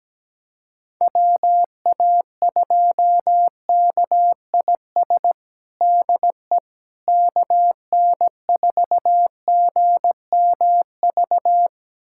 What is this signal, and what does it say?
Morse code, heard as a keyed tone, about 17 wpm, 700 Hz. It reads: WA2KIS DE KN4GMV
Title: USA Callsign DE Callsign 17wpm